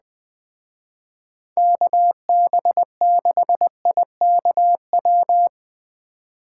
Morse code, heard as a keyed tone, about 20 wpm, 700 Hz. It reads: KB6IKW
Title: Speed Racing - USA 2x3 Call Signs 17wpm